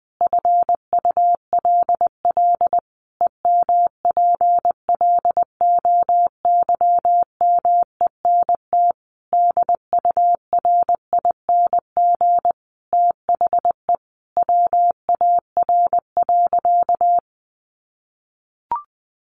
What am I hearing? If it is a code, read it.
Morse code, heard as a keyed tone, about 19 wpm, 700 Hz. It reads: FULL EMPLOYMENT DURING THE WAR.
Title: Sentences from Top 1500 Words 20wpm